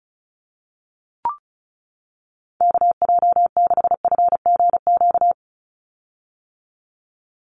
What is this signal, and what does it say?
Morse code, heard as a keyed tone, about 34 wpm, 700 Hz. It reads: KJ6FGQ
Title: USA Call Signs 35wpm